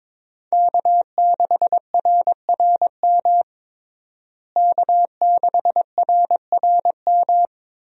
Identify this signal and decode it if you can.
Morse code, heard as a keyed tone, about 22 wpm, 700 Hz. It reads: K6RRM K6RRM
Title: USA Call Signs - 5 Characters - Repeated 2x - 22wpm